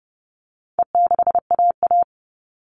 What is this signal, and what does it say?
Morse code, heard as a keyed tone, about 30 wpm, 700 Hz. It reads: E6AA
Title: All Cut Numbers - 4-Digit 30wpm